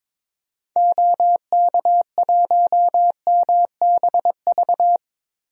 Morse code, heard as a keyed tone, about 22 wpm, 700 Hz. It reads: OK1MBV
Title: DXCC-100 Call Signs - 6 Characters - Repeated 3x - 22wpm